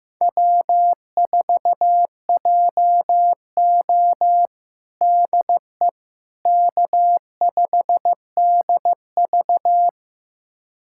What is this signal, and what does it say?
Morse code, heard as a keyed tone, about 15 wpm, 700 Hz. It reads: W4JO DE K5DV
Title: USA Callsign DE Callsign - 4 Characters - 15wpm